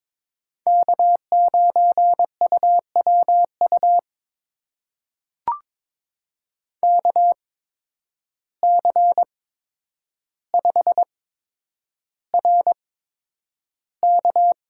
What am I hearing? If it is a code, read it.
Morse code, heard as a keyed tone, about 22 wpm, spaced as for 9 wpm, 700 Hz. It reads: K9UWU KC5RK
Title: USA Call Signs - 5 Characters - 8x Character Spacing - 22wpm